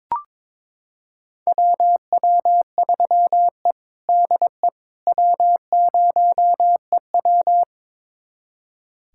Morse code, heard as a keyed tone, about 22 wpm, 700 Hz. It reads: WW3E DE W0EW
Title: USA Callsign DE Callsign - 4 Characters - 22wpm